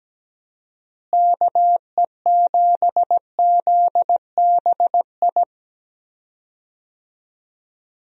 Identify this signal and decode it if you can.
Morse code, heard as a keyed tone, about 17 wpm, 700 Hz. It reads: KE7ZBI
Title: USA Call Signs - 6 Characters - Repeated 2x - 17wpm